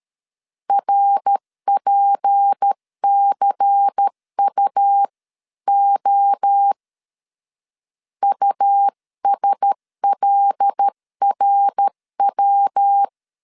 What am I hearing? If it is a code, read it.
Morse code, heard as a keyed tone, about 13 wpm, 800 Hz. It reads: RPCUO USLRW
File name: CW / DET / 1 6 1500C S C 12wpm 3533